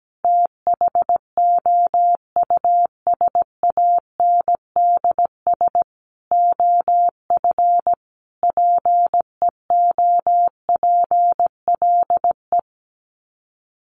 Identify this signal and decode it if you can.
Morse code, heard as a keyed tone, about 17 wpm, 700 Hz. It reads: THOUSANDS OF PEOPLE
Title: Sets of 5 Words Encore2 17wpm